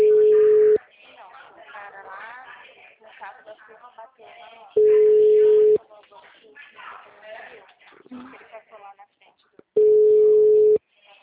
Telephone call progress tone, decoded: ringback tone

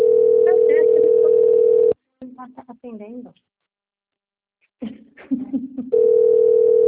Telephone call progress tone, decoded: ringback tone

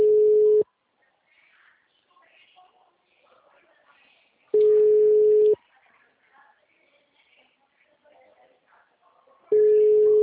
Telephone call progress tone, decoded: ringback tone

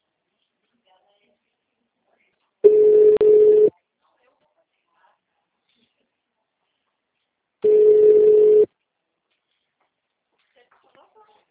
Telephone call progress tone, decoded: ringback tone

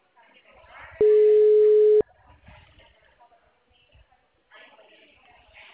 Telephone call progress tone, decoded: ringback tone

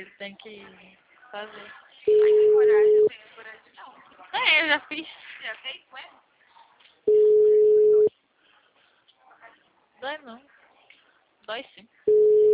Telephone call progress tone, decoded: ringback tone

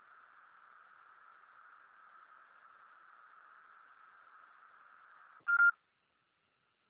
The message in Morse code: none

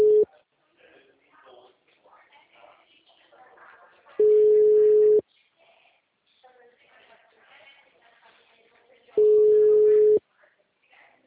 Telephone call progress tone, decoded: ringback tone

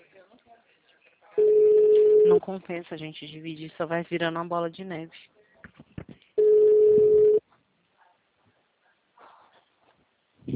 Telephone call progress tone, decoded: ringback tone

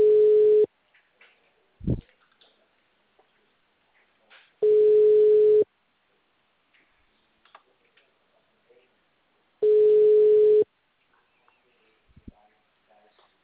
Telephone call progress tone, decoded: ringback tone

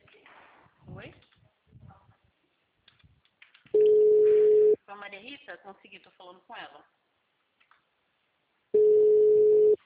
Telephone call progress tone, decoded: ringback tone

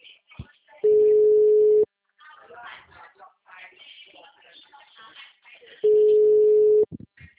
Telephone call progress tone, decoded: ringback tone

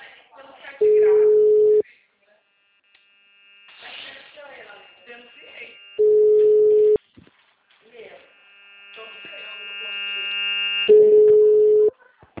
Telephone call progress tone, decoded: ringback tone

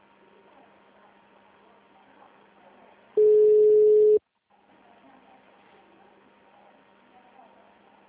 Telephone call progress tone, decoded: ringback tone

